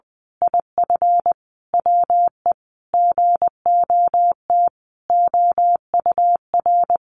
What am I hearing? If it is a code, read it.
Morse code, heard as a keyed tone, about 20 wpm, 700 Hz. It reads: IF WE GOT OUR
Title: A Sound Of Thunder 20wpm